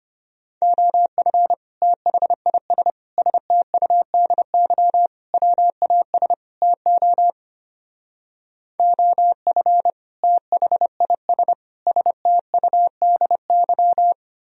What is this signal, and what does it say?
Morse code, heard as a keyed tone, about 30 wpm, 700 Hz. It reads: OF THIS STUDY WAS TO OF THIS STUDY
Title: Speed-Racing - Sets of 5 Words 25wpm